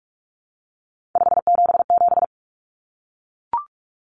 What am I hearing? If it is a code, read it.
Morse code, heard as a keyed tone, about 44 wpm, 700 Hz. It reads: 577
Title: Common Cut Numbers - 3-Digit 45wpm